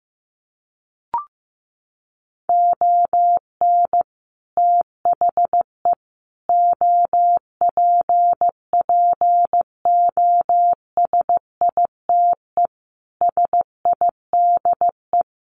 Morse code, heard as a keyed tone, about 15 wpm, 700 Hz. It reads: ON THE OPPOSITE SIDE
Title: Sets of 5 Words Encore2 15wpm